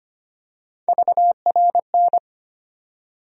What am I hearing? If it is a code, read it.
Morse code, heard as a keyed tone, about 25 wpm, 700 Hz. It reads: VRN
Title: USA Call Sign Postfixes - 3 Letters 25wpm